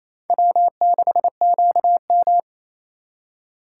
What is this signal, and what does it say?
Morse code, heard as a keyed tone, about 34 wpm, 700 Hz. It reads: W6QM
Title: Speed Racing - USA 1x2 Call Signs 28wpm